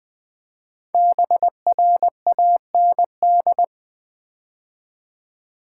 Morse code, heard as a keyed tone, about 20 wpm, 700 Hz. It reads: BRAND
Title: 5 Letter Words - First 3 Letters Given - 20wpm